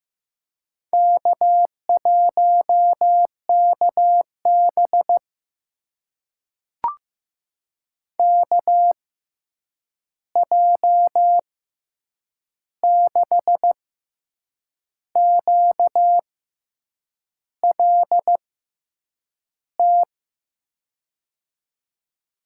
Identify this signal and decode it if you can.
Morse code, heard as a keyed tone, about 15 wpm, spaced as for 7 wpm, 700 Hz. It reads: K1KB KJ6QLT
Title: USA Call Signs - 6x Character Spacing - 15wpm